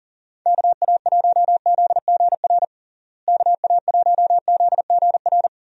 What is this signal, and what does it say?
Morse code, heard as a keyed tone, about 40 wpm, 700 Hz. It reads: KA1ZGR KA1ZGR
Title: USA Call Signs - 6 Characters - Repeated 3x - 40wpm